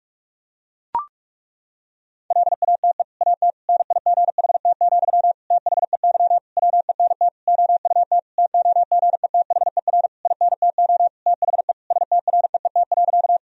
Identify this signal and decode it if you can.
Morse code, heard as a keyed tone, about 41 wpm, 700 Hz. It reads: LATE AT NIGHT, THEY WENT OUT TOGETHER INTO THE STREET.
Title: Sentences from Top 1000 Words - Review Part 3 - 45wpm